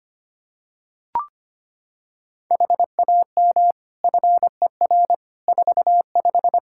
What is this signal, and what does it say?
Morse code, heard as a keyed tone, about 25 wpm, 700 Hz. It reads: HAM FER 45
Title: Practice QSO - Exchange 3 - More Info - 25wpm